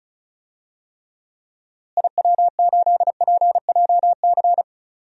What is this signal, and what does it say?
Morse code, heard as a keyed tone, about 36 wpm, 700 Hz. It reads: IW8PJC
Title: DXCC-100 Call Signs - 6 Characters - Repeated 2x - 35wpm